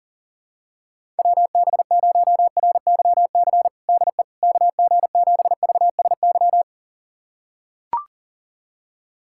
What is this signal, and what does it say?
Morse code, heard as a keyed tone, about 40 wpm, 700 Hz. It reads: WB0RYC DE KG7VSY
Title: USA Callsign DE Callsign 40wpm